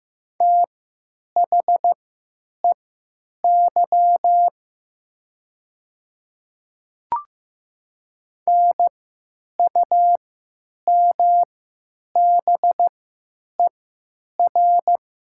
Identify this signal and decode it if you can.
Morse code, heard as a keyed tone, about 15 wpm, spaced as for 9 wpm, 700 Hz. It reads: THEY NUMBER
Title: 3x Character Spacing - Top 300 Words - 15wpm